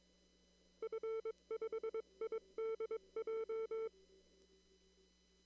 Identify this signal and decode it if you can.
Morse code, heard as a keyed tone, about 22 wpm, 450 Hz. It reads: F5IDJ
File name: CW / kt5j wpx cw-op-k5pi